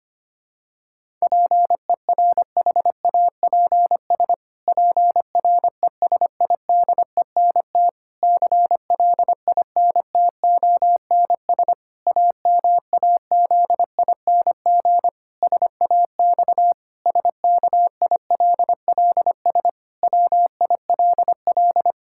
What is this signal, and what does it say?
Morse code, heard as a keyed tone, about 25 wpm, 700 Hz. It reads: PERHAPS PRESIDENT CLINTONS AMAZING SAX SKILLS WILL
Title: Pangram Challenge v2 - 25wpm